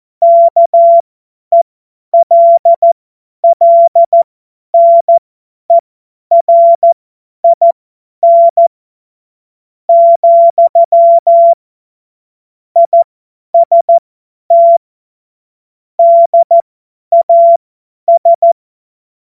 Morse code, heard as a keyed tone, about 14 wpm, spaced as for 10 wpm, 680 Hz. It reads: KELLNERIN , IST DAS
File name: Morsekurs / Witze / Witz 06. April 2021, 50 F70